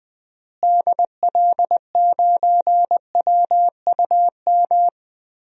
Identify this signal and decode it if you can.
Morse code, heard as a keyed tone, about 20 wpm, 700 Hz. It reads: DL9WUM
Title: DXCC-100 Call Signs - 6 Characters - Repeated 2x - 20wpm